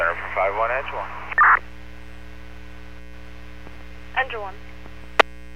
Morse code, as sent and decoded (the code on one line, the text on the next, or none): none